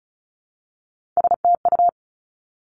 Morse code, heard as a keyed tone, about 34 wpm, 700 Hz. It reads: STU
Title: Three Challenging Letter-Number Combinations 35wpm